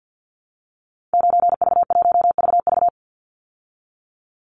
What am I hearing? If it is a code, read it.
Morse code, heard as a keyed tone, about 50 wpm, 700 Hz. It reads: 9V1VV
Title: Call Signs Repeated 2x - 50wpm